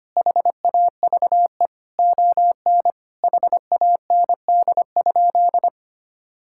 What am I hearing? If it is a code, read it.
Morse code, heard as a keyed tone, about 25 wpm, 700 Hz. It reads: HAVE ON HAND?
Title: Sentences from Top 1000 Words - Review Part 5 - 25wpm